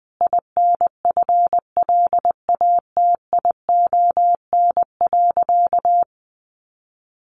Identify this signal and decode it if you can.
Morse code, heard as a keyed tone, about 20 wpm, 700 Hz. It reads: INFLATION.
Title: Sentences from Top 1100 Words 20wpm